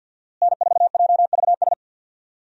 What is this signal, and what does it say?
Morse code, heard as a keyed tone, about 51 wpm, 700 Hz. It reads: N4JVS
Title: USA Call Signs - 5 Characters - Repeated 3x - 50wpm